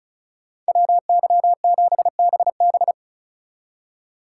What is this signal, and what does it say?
Morse code, heard as a keyed tone, about 45 wpm, 700 Hz. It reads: WY7BB